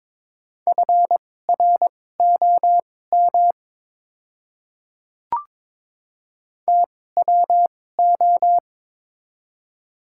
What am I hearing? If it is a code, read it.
Morse code, heard as a keyed tone, about 22 wpm, spaced as for 16 wpm, 700 Hz. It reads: FROM TWO
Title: 2x Character Spacing - Top 100 Words - 22wpm